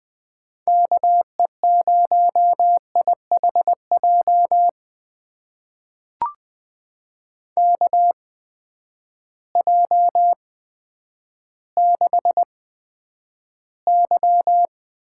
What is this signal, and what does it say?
Morse code, heard as a keyed tone, about 20 wpm, spaced as for 10 wpm, 700 Hz. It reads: KE0IHJ KJ6Y
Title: USA Call Signs - 6 Characters - 8x Character Spacing - 20wpm